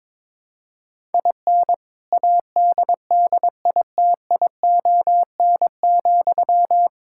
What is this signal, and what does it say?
Morse code, heard as a keyed tone, about 22 wpm, 700 Hz. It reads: IN ADDITION,
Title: Sentences from Top 1000 Words - Review Part 6 - 22wpm